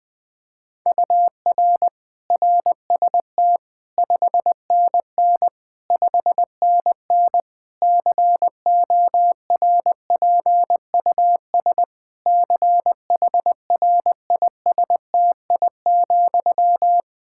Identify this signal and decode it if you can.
Morse code, heard as a keyed tone, about 20 wpm, 700 Hz. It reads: UR RST 5NN 5NN CORPUS CHRISTI,
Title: K1USN SST QSO Practice 20wpm